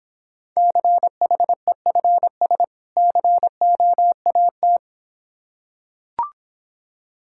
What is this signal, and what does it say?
Morse code, heard as a keyed tone, about 26 wpm, 700 Hz. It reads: CHEFS COAT